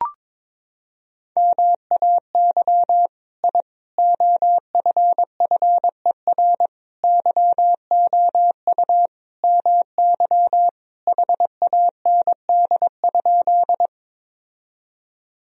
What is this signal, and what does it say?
Morse code, heard as a keyed tone, about 22 wpm, 700 Hz. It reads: MAY I OFFER YOU MY HAND?
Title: Sentences from Top 1000 Words - Review Part 4 - 22wpm